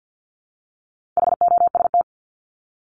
Morse code, heard as a keyed tone, about 50 wpm, 700 Hz. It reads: HOST
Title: Top 1501-1750 Words 50wpm